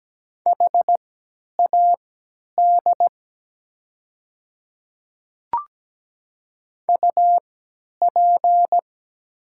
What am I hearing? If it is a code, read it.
Morse code, heard as a keyed tone, about 17 wpm, spaced as for 10 wpm, 700 Hz. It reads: HAD UP
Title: 3x Character Spacing - Top 100 Words - 17wpm